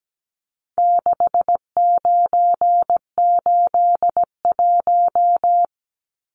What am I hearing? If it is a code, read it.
Morse code, heard as a keyed tone, about 17 wpm, 700 Hz. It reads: 6981
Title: Numbers - 4-Digit 17wpm